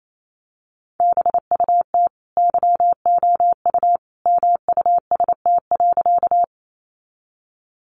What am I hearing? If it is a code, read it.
Morse code, heard as a keyed tone, about 28 wpm, 700 Hz. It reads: BUT YOU MUST.
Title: Sentences from Top 200 Words 28wpm